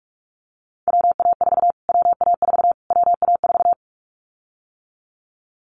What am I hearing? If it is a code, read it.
Morse code, heard as a keyed tone, about 44 wpm, 700 Hz. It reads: WA4 WA4 WA4
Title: Callsign Prefixes for Top 25 DXCC - Repeated 3x - 45wpm